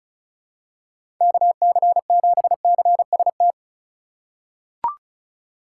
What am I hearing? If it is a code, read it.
Morse code, heard as a keyed tone, about 34 wpm, 700 Hz. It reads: KC7CST
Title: USA Call Signs - 8x Character Spacing - 35wpm